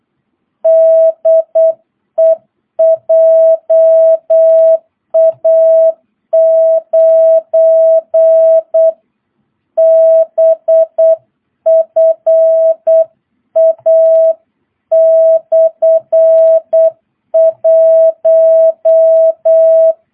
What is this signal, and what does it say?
Morse code, heard as a keyed tone, about 8 wpm, 650 Hz. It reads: DE JA9BFA/1
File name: jh8cbh / 2022cw